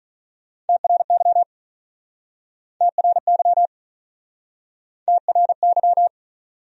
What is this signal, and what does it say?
Morse code, heard as a keyed tone, about 41 wpm, 700 Hz. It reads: TRY TRY TRY
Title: Speed-Racing Top 300 Words 35wpm